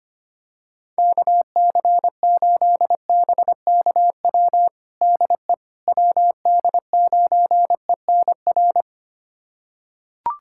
Morse code, heard as a keyed tone, about 25 wpm, 700 Hz. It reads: KC8BKW DE WD9ENR
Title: USA Callsign DE Callsign - 6 Characters - 25wpm